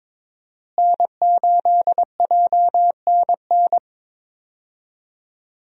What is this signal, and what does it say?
Morse code, heard as a keyed tone, about 22 wpm, 700 Hz. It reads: N8JNN
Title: USA Call Signs - 5 Characters - Repeated 2x - 22wpm